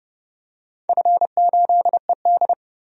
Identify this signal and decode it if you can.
Morse code, heard as a keyed tone, about 30 wpm, 700 Hz. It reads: F8ED
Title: DXCC-100 Call Signs - 4 Characters - 30wpm